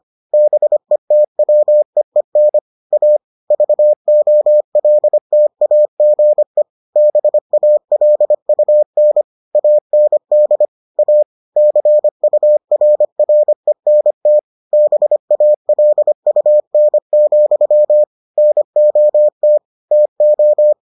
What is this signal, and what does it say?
Morse code, heard as a keyed tone, about 25 wpm, 600 Hz. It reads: BETWEEN A VOLTAGE BALUN AND A CURRENT BALUN, NOT TO